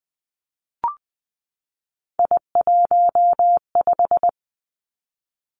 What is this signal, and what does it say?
Morse code, heard as a keyed tone, about 20 wpm, 700 Hz. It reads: I15
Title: Three Challenging Letter-Number Combinations 20wpm